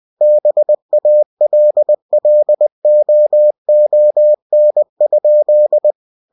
Morse code, heard as a keyed tone, about 20 wpm, 600 Hz. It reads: BALLOON?
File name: cw-books / winnie-the-pooh / winnie-20wpm-page-0006